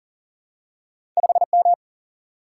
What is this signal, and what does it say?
Morse code, heard as a keyed tone, about 42 wpm, 700 Hz. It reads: 5M
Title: Single-Letter Single-Number 40wpm